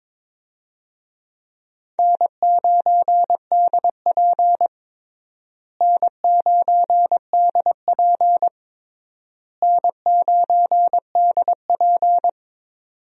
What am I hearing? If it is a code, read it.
Morse code, heard as a keyed tone, about 22 wpm, 700 Hz. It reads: N9DP N9DP N9DP
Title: USA Call Signs - 4 Characters - Repeated 3x - 22wpm